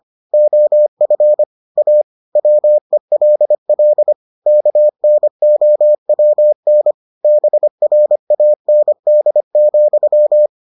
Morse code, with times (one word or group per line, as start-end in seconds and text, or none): 0.3-1.4 OF
1.8-2.0 A
2.4-4.1 WELL
4.5-6.9 KNOWN
7.2-10.5 BRAND,